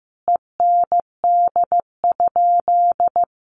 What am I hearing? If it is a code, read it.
Morse code, heard as a keyed tone, about 15 wpm, 700 Hz. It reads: END?
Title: Sentences from Top 1200 Words 15wpm